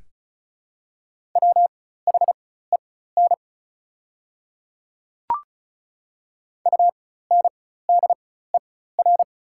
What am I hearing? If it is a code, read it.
Morse code, heard as a keyed tone, about 36 wpm, spaced as for 16 wpm, 690 Hz. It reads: WHEN UNDER
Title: 4x Character Spacing - Top 200 Words - 35wpm